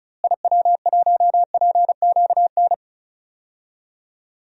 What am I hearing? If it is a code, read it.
Morse code, heard as a keyed tone, about 34 wpm, 700 Hz. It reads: IW1PQN
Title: DXCC-100 Call Signs - 6 Characters - Repeated 2x - 35wpm